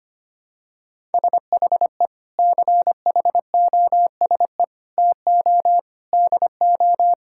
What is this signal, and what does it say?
Morse code, heard as a keyed tone, about 25 wpm, 700 Hz. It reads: SHE CHOSE TO DO
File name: practice / 25wpm / Sentences with 3-Word Phrases - 25wpm